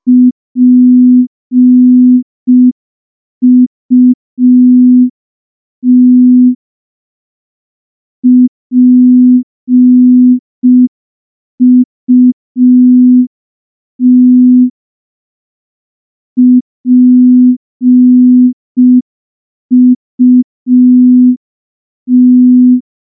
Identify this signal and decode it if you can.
Morse code, heard as a keyed tone, about 5 wpm, 250 Hz. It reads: PUT PUT PUT
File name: morse / morse-training-05WPM